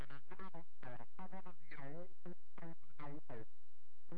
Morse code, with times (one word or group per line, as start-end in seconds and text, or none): none